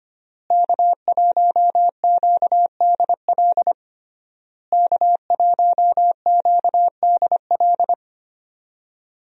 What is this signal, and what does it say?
Morse code, heard as a keyed tone, about 25 wpm, 700 Hz. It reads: K1QDL K1QDL
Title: USA Call Signs - 5 Characters - Repeated 3x - 25wpm